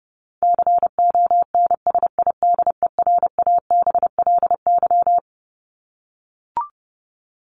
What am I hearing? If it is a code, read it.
Morse code, heard as a keyed tone, about 30 wpm, 700 Hz. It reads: CONSIDERABLY
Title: LY Suffix - Top 100 Words with Base Words 30wpm